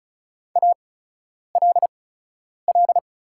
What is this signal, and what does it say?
Morse code, heard as a keyed tone, about 36 wpm, spaced as for 10 wpm, 700 Hz. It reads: ALL
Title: 8x Character Spacing - Top 200 Words - 35wpm